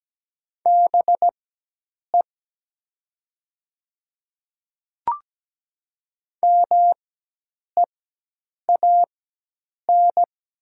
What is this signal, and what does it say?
Morse code, heard as a keyed tone, about 17 wpm, spaced as for 8 wpm, 700 Hz. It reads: BE MEAN
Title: 4x Character Spacing - Top 200 Words - 17wpm